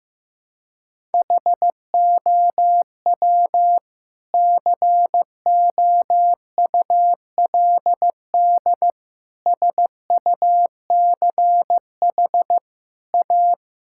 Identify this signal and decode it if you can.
Morse code, heard as a keyed tone, about 15 wpm, 700 Hz. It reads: HOW COULD SUCH A
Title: Sentences from Top 1000 Words - Review Part 6 - 15wpm